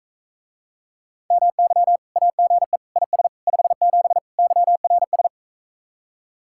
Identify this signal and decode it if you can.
Morse code, heard as a keyed tone, about 44 wpm, 700 Hz. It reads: MY AGE IS 57 YRS